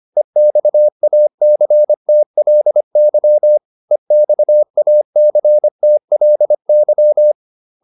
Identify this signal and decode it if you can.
Morse code, heard as a keyed tone, about 25 wpm, 600 Hz. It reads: EXACTLY EXACTLY